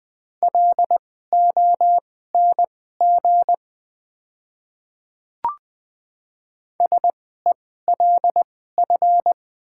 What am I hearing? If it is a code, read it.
Morse code, heard as a keyed tone, about 20 wpm, spaced as for 14 wpm, 700 Hz. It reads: LONG SELF
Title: 2x Character Spacing - Top 200 Words - 20wpm